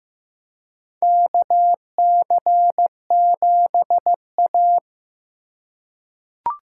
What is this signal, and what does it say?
Morse code, heard as a keyed tone, about 15 wpm, 700 Hz. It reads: KC7A
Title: USA Call Signs - 4 Characters - 3x Character Spacing - 15wpm